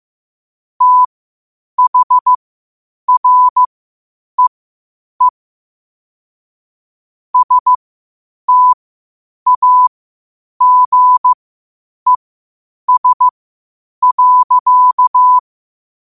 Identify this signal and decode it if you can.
Morse code, heard as a keyed tone, about 15 wpm, spaced as for 8 wpm, 1000 Hz. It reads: THREE STAGES.
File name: qotdfiles / 8wpm